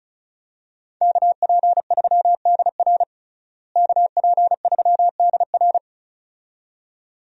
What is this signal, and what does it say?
Morse code, heard as a keyed tone, about 34 wpm, 700 Hz. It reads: KP3DR KP3DR
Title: USA Call Signs - 5 Characters - Repeated 2x - 35wpm